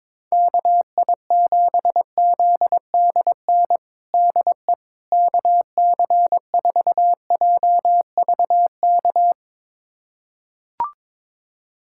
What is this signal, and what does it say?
Morse code, heard as a keyed tone, about 22 wpm, 700 Hz. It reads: KI7ZDN DE KC4JVK
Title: USA Callsign DE Callsign - 6 Characters - 22wpm